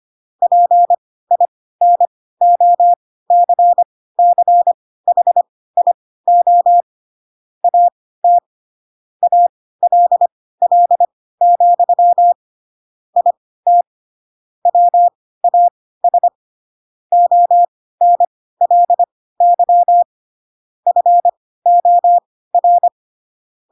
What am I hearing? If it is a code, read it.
Morse code, heard as a keyed tone, about 25 wpm, spaced as for 16 wpm, 700 Hz. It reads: PINOCCHIO AT ALL, IT WAS ONLY FOR